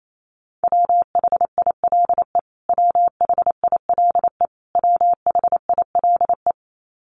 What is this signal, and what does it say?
Morse code, heard as a keyed tone, about 28 wpm, 700 Hz. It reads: WHILE WHILE WHILE
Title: Top 300 Words Repeated 28wpm